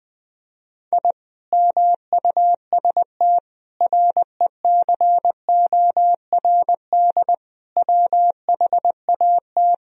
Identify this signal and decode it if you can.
Morse code, heard as a keyed tone, about 20 wpm, 700 Hz. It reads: I MUST RECORD WHAT
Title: Sentences from Top 1000 Words - Review Part 3 - 20wpm